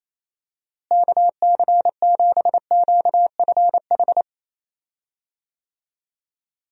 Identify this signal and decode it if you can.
Morse code, heard as a keyed tone, about 28 wpm, 700 Hz. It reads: KC7QFH
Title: USA Call Signs - 6 Characters - Repeated 3x - 28wpm